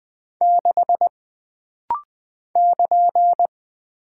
Morse code, heard as a keyed tone, about 20 wpm, 700 Hz. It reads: 6 <KN>
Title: Single Character (Letter:Number:Punctuation:Prosign) - Rapid-Fire - 20wpm